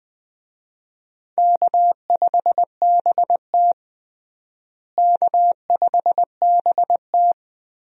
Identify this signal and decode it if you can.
Morse code, heard as a keyed tone, about 20 wpm, 700 Hz. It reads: K5BT K5BT
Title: USA Call Signs - 4 Characters - Repeated 3x - 20wpm